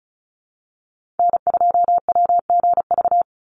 Morse code, heard as a keyed tone, about 34 wpm, 700 Hz. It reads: N2WGV
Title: USA Call Signs 35wpm